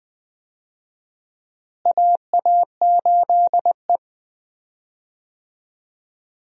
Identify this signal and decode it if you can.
Morse code, heard as a keyed tone, about 20 wpm, 700 Hz. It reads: AA8E
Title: USA Call Signs - 4 Characters - Repeated 2x - 20wpm